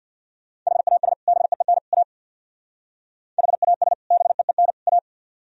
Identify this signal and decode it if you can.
Morse code, heard as a keyed tone, about 64 wpm, 700 Hz. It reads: HAS BEEN A HAS BEEN A